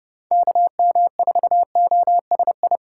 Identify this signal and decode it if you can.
Morse code, heard as a keyed tone, about 32 wpm, 700 Hz. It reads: KM4OSI
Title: USA Call Signs - 6 Characters - 2x Character Spacing - 30wpm